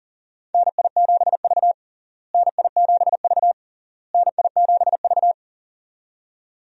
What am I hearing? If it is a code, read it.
Morse code, heard as a keyed tone, about 40 wpm, 700 Hz. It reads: NI7V NI7V NI7V
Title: USA Call Signs - 4 Characters - Repeated 3x - 40wpm